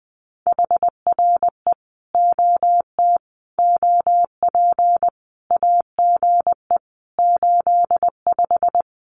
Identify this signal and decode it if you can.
Morse code, heard as a keyed tone, about 20 wpm, 700 Hz. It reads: HRE OT OP AGE 85
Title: Common QSO Phrases 20wpm